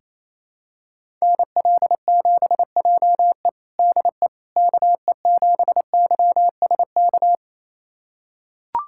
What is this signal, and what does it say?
Morse code, heard as a keyed tone, about 27 wpm, 700 Hz. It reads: NL7JE DE KE7YSK
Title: USA Callsign DE Callsign 28wpm